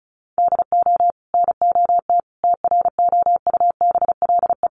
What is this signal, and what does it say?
Morse code, heard as a keyed tone, about 34 wpm, 700 Hz. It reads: DO NOT TROUBLE
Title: Sentences from Top 1200 Words 35wpm